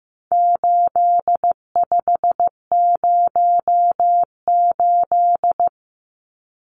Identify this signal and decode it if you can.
Morse code, heard as a keyed tone, about 15 wpm, 700 Hz. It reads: 8508
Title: Numbers - 4-Digit 15wpm